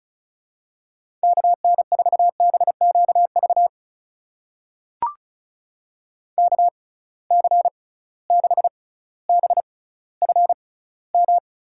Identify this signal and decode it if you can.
Morse code, heard as a keyed tone, about 36 wpm, spaced as for 18 wpm, 700 Hz. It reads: KN4BQV KC6BFM
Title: USA Call Signs - 6 Characters - 6x Character Spacing - 35wpm